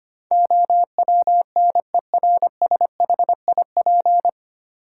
Morse code, heard as a keyed tone, about 25 wpm, 700 Hz. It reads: OWNERSHIP
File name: Nine Letter Words - 25wpm